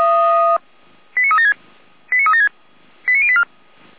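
Five-tone selective call (ZVEI), heard as quick runs of five tones, tones of about 70 ms. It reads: 89376 89376 89084
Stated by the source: ZVEI five-tone selective call received on an SDR.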